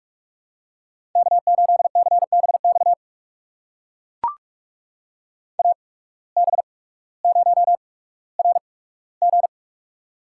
Morse code, heard as a keyed tone, about 44 wpm, spaced as for 19 wpm, 700 Hz. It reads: K8CBX AB0RG
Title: USA Call Signs - 5 Characters - 8x Character Spacing - 45wpm